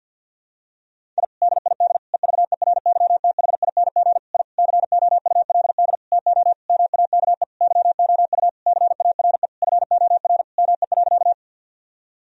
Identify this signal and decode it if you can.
Morse code, heard as a keyed tone, about 47 wpm, 700 Hz. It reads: I DID EVERYTHING I COULD TO MAKE YOU CARE FOR ME.
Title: Sentences from Top 1000 Words - Review Part 1 - 50wpm